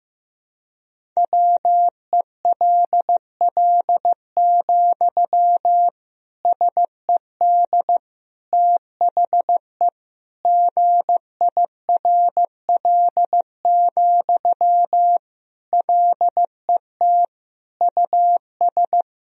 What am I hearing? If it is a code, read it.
Morse code, heard as a keyed tone, about 15 wpm, 700 Hz. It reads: WELL, SED THE GIRL, LET US
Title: Sentences from Top 400 Words and Abbreviations 15wpm